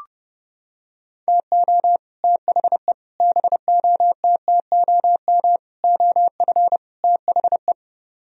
Morse code, heard as a keyed tone, about 29 wpm, 700 Hz. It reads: TO THE BOTTOM OF THE